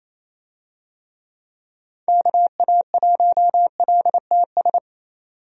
Morse code, heard as a keyed tone, about 28 wpm, 700 Hz. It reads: KA1LTS
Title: USA Call Signs - 6 Characters - 28wpm